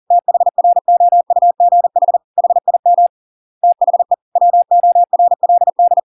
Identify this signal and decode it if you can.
Morse code, heard as a keyed tone, about 40 wpm, 700 Hz. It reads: THROUGH HIM THE WORLD